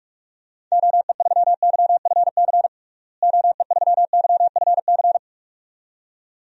Morse code, heard as a keyed tone, about 42 wpm, 700 Hz. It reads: OE3YFC OE3YFC
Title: DXCC-100 Call Signs - 6 Characters - Repeated 2x - 45wpm